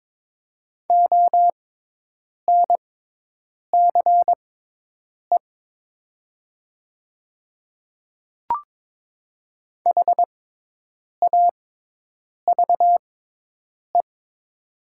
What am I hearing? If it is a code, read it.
Morse code, heard as a keyed tone, about 22 wpm, spaced as for 8 wpm, 700 Hz. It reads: ONCE HAVE
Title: 6x Character Spacing - Top 500 Words - 22wpm